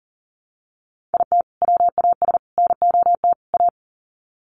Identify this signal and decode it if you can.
Morse code, heard as a keyed tone, about 40 wpm, 700 Hz. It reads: IT WAS NOT A
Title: Sets of 4 Words 40wpm